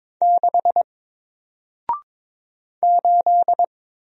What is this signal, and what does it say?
Morse code, heard as a keyed tone, about 22 wpm, 700 Hz. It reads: 6 8
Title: Single Letter-Number Alt 22wpm